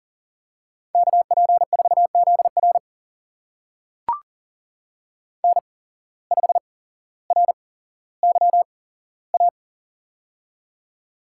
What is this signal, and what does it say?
Morse code, heard as a keyed tone, about 42 wpm, spaced as for 17 wpm, 700 Hz. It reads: KP4ZR N5RYA
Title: USA Call Signs - 8x Character Spacing - 40wpm